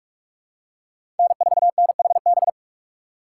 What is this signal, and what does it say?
Morse code, heard as a keyed tone, about 46 wpm, 700 Hz. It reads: N4NHB
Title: USA Call Signs - 5 Characters - 3x Character Spacing - 45wpm